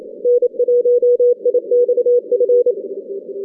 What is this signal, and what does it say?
Morse code, heard as a keyed tone, about 29 wpm, 500 Hz. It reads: N1IXF
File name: 2015 cq ww cw / 2015 cqwww k5tr - 0-18